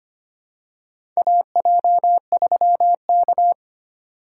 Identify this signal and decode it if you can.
Morse code, heard as a keyed tone, about 25 wpm, 700 Hz. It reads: AJ3K
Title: USA Call Signs - 4 Characters - 2x Character Spacing - 25wpm